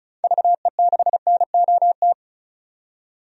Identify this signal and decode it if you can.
Morse code, heard as a keyed tone, about 35 wpm, 700 Hz. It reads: VE6NOT
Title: DXCC-100 Call Signs - 6 Characters - 35wpm